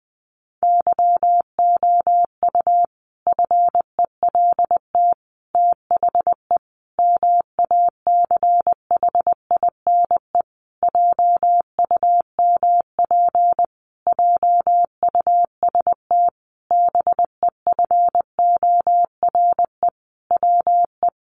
Morse code, heard as a keyed tone, about 20 wpm, 700 Hz. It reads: YOU FELT THE MACHINE JUMP JUST BEFORE WE
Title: A Sound Of Thunder 20wpm